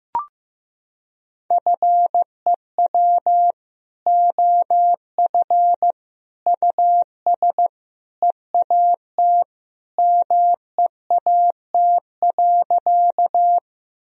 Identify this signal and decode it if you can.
Morse code, heard as a keyed tone, about 15 wpm, 700 Hz. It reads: FEW OF US EAT MEAT.
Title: Sentences from Top 1000 Words - Review Part 3 - 15wpm